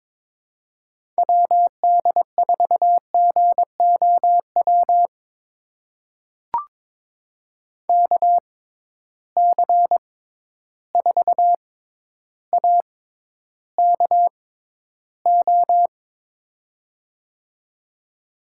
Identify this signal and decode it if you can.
Morse code, heard as a keyed tone, about 22 wpm, spaced as for 11 wpm, 700 Hz. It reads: WD4GOW KC4AKO
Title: USA Call Signs - 6 Characters - 6x Character Spacing - 22wpm